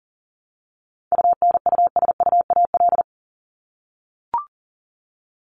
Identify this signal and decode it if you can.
Morse code, heard as a keyed tone, about 40 wpm, 700 Hz. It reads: UNUSUAL
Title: Top 1251-1500 Words 40wpm